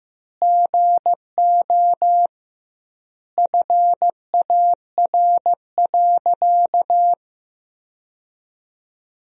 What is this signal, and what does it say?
Morse code, heard as a keyed tone, about 15 wpm, 700 Hz. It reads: GO FAR.
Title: 2x Word-Spacing - Sentences from Top 400 Words - 15wpm